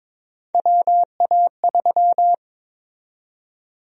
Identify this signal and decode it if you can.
Morse code, heard as a keyed tone, about 22 wpm, 700 Hz. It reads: WA3
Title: USA Call Sign Prefixes - 22wpm